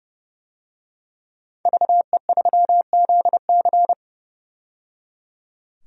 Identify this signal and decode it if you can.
Morse code, heard as a keyed tone, about 30 wpm, 700 Hz. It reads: VE3ZC
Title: DXCC-100 Call Signs - 5 Characters - 30wpm